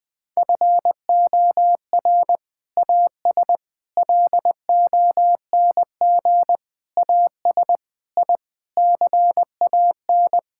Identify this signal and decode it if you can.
Morse code, heard as a keyed tone, about 20 wpm, 700 Hz. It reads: FOR AS LONG AS I CAN
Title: Sentences with 8-Word Phrases - Spoken Beginning - 20wpm